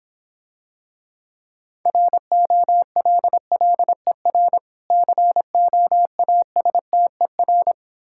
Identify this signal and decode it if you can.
Morse code, heard as a keyed tone, about 26 wpm, 700 Hz. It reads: ROLLER COASTER